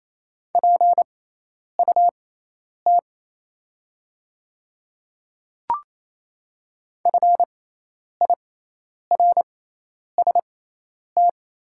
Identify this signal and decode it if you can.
Morse code, heard as a keyed tone, about 28 wpm, spaced as for 10 wpm, 700 Hz. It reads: PUT FIRST
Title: 6x Character Spacing - Top 500 Words - 28wpm